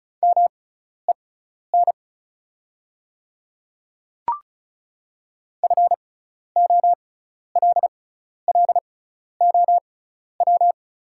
Morse code, heard as a keyed tone, about 36 wpm, spaced as for 12 wpm, 690 Hz. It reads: MEN FOLLOW
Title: 6x Character Spacing - Top 300 Words - 35wpm